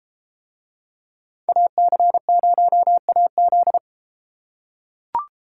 Morse code, heard as a keyed tone, about 33 wpm, 700 Hz. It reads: AC0AZ